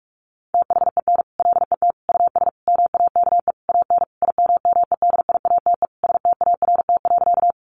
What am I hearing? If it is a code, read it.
Morse code, heard as a keyed tone, about 44 wpm, 700 Hz. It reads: THEN LET US MAKE AN IMMEDIATE START.
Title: Sentences from Top 1200 Words 45wpm